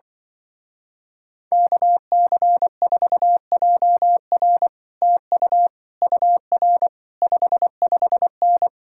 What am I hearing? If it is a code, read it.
Morse code, heard as a keyed tone, about 24 wpm, 700 Hz. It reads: KC4JR TU UR 55N
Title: Speed-Racing Common QSO Phrases Encore - 20wpm